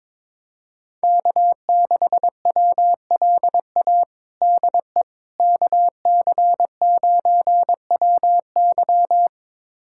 Morse code, heard as a keyed tone, about 22 wpm, 700 Hz. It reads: K6WLA DE KC9WY
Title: USA Callsign DE Callsign 22wpm